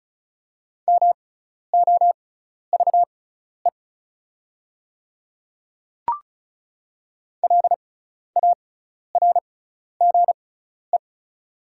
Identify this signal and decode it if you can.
Morse code, heard as a keyed tone, about 36 wpm, spaced as for 12 wpm, 700 Hz. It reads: MOVE LARGE
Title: 6x Character Spacing - Top 300 Words - 35wpm